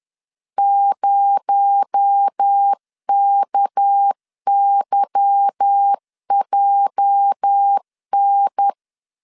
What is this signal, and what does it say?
Morse code, heard as a keyed tone, about 10 wpm, 800 Hz. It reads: 0KYJN
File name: LEZIONI / 8 YJ45 / 8 YJ 10wpm 2651